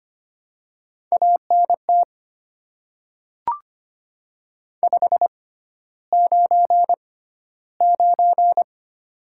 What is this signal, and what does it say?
Morse code, heard as a keyed tone, about 25 wpm, spaced as for 13 wpm, 700 Hz. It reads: ANT 599
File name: Top 90 QSO Elements - 6x Character Spacing - 25wpm